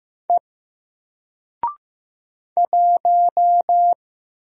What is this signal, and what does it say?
Morse code, heard as a keyed tone, about 15 wpm, 700 Hz. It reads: E 1
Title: Single Character (Letter:Number:Punctuation:Prosign) 15wpm